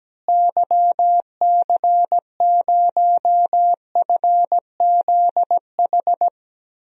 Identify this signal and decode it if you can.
Morse code, heard as a keyed tone, about 17 wpm, 700 Hz. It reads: YC0FZH
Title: DXCC-100 Call Signs - 6 Characters - Repeated 2x - 17wpm